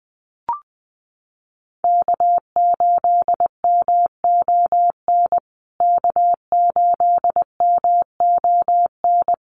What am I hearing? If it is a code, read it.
Morse code, heard as a keyed tone, about 20 wpm, 700 Hz. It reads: K8MON K8MON
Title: USA Call Signs Repeated 2x 20wpm